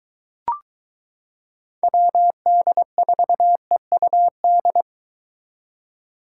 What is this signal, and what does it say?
Morse code, heard as a keyed tone, about 23 wpm, 700 Hz. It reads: WD4EUD